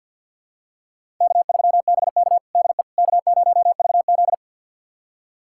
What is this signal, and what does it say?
Morse code, heard as a keyed tone, about 46 wpm, 700 Hz. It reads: K3BK DE K0VZ
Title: USA Callsign DE Callsign - 4 Characters - 50wpm